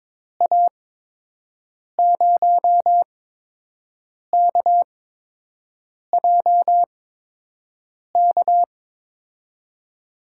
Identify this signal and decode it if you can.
Morse code, heard as a keyed tone, about 22 wpm, spaced as for 6 wpm, 700 Hz. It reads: A0KJK
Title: USA Call Signs - 6 Characters - 8x Character Spacing - 22wpm